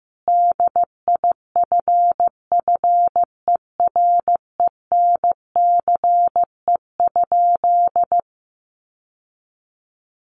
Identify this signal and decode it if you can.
Morse code, heard as a keyed tone, about 14 wpm, 700 Hz. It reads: DIFFERENCE?
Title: Sentences from Top 800 Words 15wpm